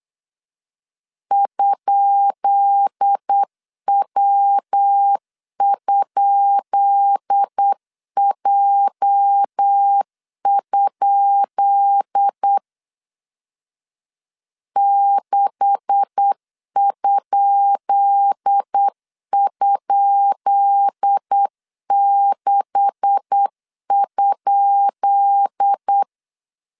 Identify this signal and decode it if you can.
Morse code, heard as a keyed tone, about 8 wpm, 800 Hz. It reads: ?W?J? 6??6?
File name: LEZIONI / 10 SIGNS / 10B INT 8wpm 3534